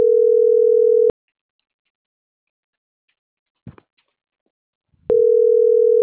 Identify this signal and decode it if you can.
Ringback tone (North America), 440 + 480 Hz.